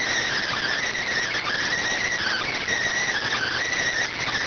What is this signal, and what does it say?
Emergency Alert System (SAME) tones, heard as harsh buzzing bursts. It is heard from 0.5 to 3.8 s.